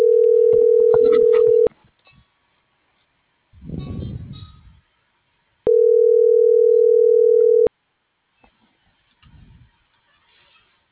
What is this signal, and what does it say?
Ringback tone (North America), 440 + 480 Hz.